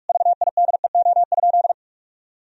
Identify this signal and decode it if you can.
Morse code, heard as a keyed tone, about 46 wpm, 700 Hz. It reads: VIDEO?
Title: Sentences from Top 1000 Words - Review Part 4 - 45wpm